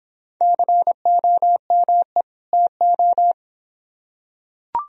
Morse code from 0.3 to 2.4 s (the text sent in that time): COME